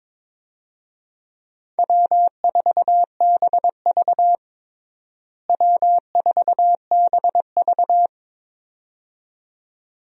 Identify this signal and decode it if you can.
Morse code, heard as a keyed tone, about 22 wpm, 700 Hz. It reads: W4BV W4BV
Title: USA Call Signs - 4 Characters - Repeated 2x - 22wpm